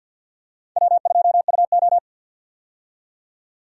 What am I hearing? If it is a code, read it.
Morse code, heard as a keyed tone, about 45 wpm, 700 Hz. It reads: W2UO
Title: USA Call Signs - 4 Characters - 50wpm